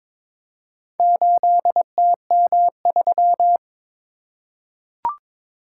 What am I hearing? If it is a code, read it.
Morse code, heard as a keyed tone, about 22 wpm, 700 Hz. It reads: 8TM3
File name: practice / 22wpm / Four Letter-Number Combinations - 22wpm